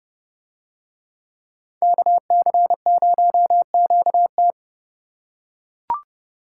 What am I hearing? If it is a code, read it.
Morse code, heard as a keyed tone, about 30 wpm, 700 Hz. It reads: KC0QT